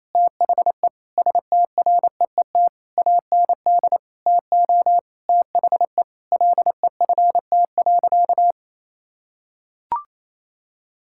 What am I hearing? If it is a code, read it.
Morse code, heard as a keyed tone, about 28 wpm, 700 Hz. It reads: THE STREET AND TO THE LEFT.
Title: Sentences from Top 1600 Words 28wpm